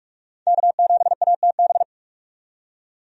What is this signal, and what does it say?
Morse code, heard as a keyed tone, about 44 wpm, 700 Hz. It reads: K7ATB
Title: USA Call Signs - 5 Characters - 45wpm